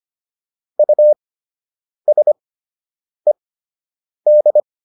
Morse code, heard as a keyed tone, about 25 wpm, spaced as for 8 wpm, 600 Hz. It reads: USED
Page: fundamental-practice-files